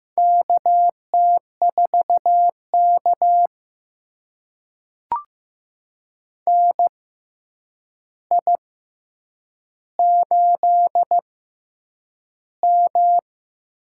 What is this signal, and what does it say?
Morse code, heard as a keyed tone, about 15 wpm, spaced as for 8 wpm, 700 Hz. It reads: KT4K NI8M
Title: USA Call Signs - 4 Characters - 6x Character Spacing - 15wpm